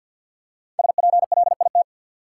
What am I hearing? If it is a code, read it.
Morse code, heard as a keyed tone, about 50 wpm, 700 Hz. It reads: SPLIT
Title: Five Letter Words - 50wpm